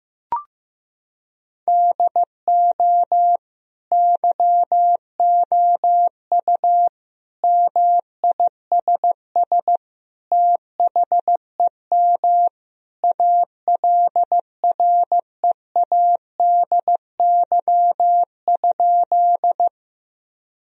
Morse code, heard as a keyed tone, about 14 wpm, 700 Hz. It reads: DO YOU MISS THEM ALREADY?
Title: Sentences from Top 1000 Words - Review Part 3 - 15wpm